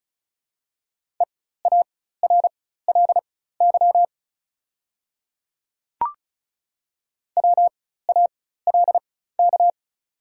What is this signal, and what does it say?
Morse code, heard as a keyed tone, about 34 wpm, spaced as for 16 wpm, 690 Hz. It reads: EARLY WALK
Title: 4x Character Spacing - Top 500 Words - 35wpm